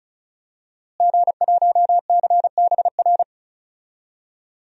Morse code, heard as a keyed tone, about 36 wpm, 700 Hz. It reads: G1CBR
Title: DXCC-100 Call Signs - 5 Characters - Repeated 3x - 35wpm